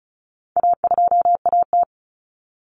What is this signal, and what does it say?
Morse code, heard as a keyed tone, about 34 wpm, 700 Hz. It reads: A2AT
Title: Common Cut Numbers - 4-Digit 35wpm